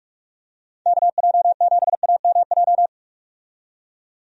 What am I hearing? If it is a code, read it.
Morse code, heard as a keyed tone, about 46 wpm, 700 Hz. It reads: KJ7AMJ
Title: USA Call Signs - 6 Characters - 45wpm